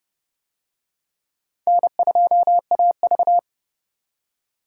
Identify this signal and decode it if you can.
Morse code, heard as a keyed tone, about 29 wpm, 700 Hz. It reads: N2AV